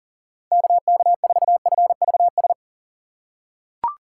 Morse code, heard as a keyed tone, about 37 wpm, 700 Hz. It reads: KK4FVS
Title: USA Call Signs - 3x Character Spacing - 40wpm